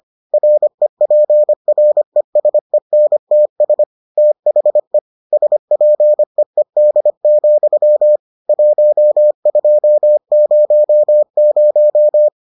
Morse code, heard as a keyed tone, about 25 wpm, 600 Hz. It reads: REPRESENTS THE SPEED, 1200